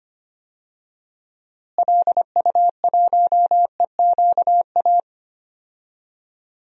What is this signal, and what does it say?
Morse code, heard as a keyed tone, about 25 wpm, 700 Hz. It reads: LU1EQA
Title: DXCC-100 Call Signs - 6 Characters - Repeated 3x - 25wpm